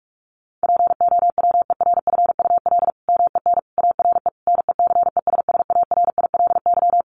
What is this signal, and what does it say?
Morse code, heard as a keyed tone, about 44 wpm, 700 Hz. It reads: POWERFUL MEN ARE NECESSARILY
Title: Sentences from Top 1400 Words 45wpm